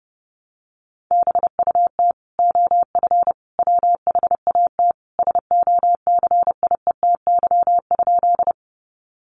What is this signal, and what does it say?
Morse code, heard as a keyed tone, about 28 wpm, 700 Hz. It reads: BUT OF WHAT SOCIETY?
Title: Sentences from Top 700 Words 30wpm